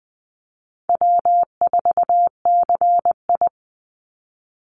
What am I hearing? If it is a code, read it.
Morse code, heard as a keyed tone, about 20 wpm, 700 Hz. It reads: W4CI
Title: Call Signs Repeated 2x - 20wpm